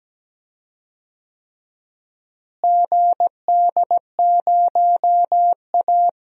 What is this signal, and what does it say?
Morse code, heard as a keyed tone, about 17 wpm, 700 Hz. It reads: GD0A
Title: DXCC-100 Call Signs - 4 Characters - Repeated 3x - 17wpm